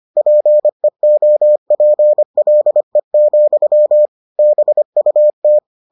Morse code, heard as a keyed tone, about 25 wpm, 600 Hz. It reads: PEOPLE, BUT